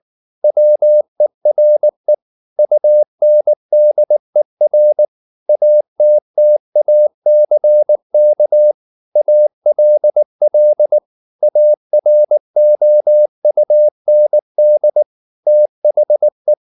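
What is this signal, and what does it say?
Morse code, heard as a keyed tone, about 19 wpm, 600 Hz. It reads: WERE UNDER ATTACK ALL AROUND THE
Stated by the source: Morse code at 19 wpm from CW med Gnister, reading Ken Follett's novel Never in English.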